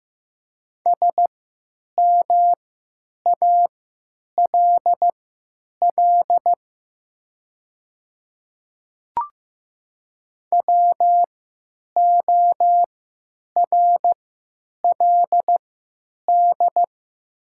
Morse code, heard as a keyed tone, about 15 wpm, spaced as for 9 wpm, 700 Hz. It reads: SMALL WORLD
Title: 3x Character Spacing - Top 200 Words - 15wpm